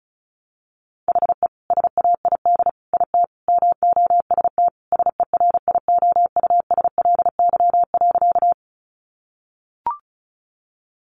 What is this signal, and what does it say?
Morse code, heard as a keyed tone, about 34 wpm, 700 Hz. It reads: HE SAID IT MOST SERIOUSLY.
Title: Sentences from Top 1200 Words 35wpm